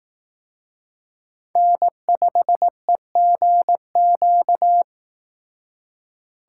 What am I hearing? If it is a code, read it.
Morse code, heard as a keyed tone, about 18 wpm, 700 Hz. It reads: N5EGQ
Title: Speed Racing - USA 1x3 Call Signs 15wpm